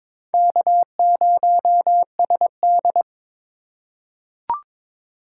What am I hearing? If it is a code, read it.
Morse code, heard as a keyed tone, about 22 wpm, 700 Hz. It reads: K0SD
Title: USA Call Signs - 4 Characters - 4x Character Spacing - 22wpm